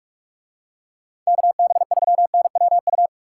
Morse code, heard as a keyed tone, about 44 wpm, 700 Hz. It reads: KB3NWU